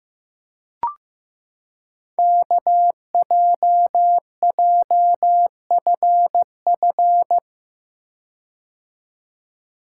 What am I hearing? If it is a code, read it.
Morse code, heard as a keyed tone, about 15 wpm, 700 Hz. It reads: KJJFF
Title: Code Groups - 5 Letters - 15wpm